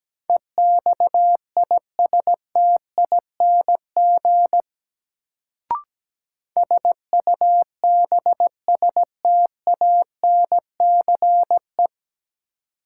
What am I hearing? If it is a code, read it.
Morse code, heard as a keyed tone, about 17 wpm, 700 Hz. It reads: EXISTING SUBSTANCE
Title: Sentences from Top 1500 Words - New Word Drill Alt - 17wpm